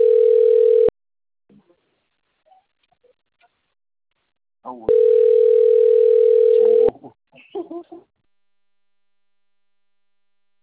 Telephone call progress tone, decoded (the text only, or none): ringback tone